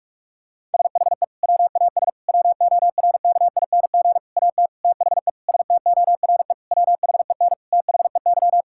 Morse code, heard as a keyed tone, about 47 wpm, 700 Hz. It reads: SHE WAS WORKING AT THE STORE WHEN THEY
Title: Sentences from Top 1000 Words - Review Part 5 - 45wpm